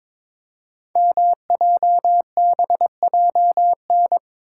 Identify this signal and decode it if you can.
Morse code, heard as a keyed tone, about 22 wpm, 700 Hz. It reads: MJBJN
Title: Code Groups - 5 Letters - 22wpm